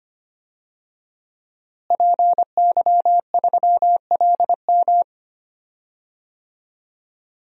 Morse code, heard as a keyed tone, about 25 wpm, 700 Hz. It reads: PY3LM
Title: DXCC-100 Call Signs - 5 Characters - Repeated 3x - 25wpm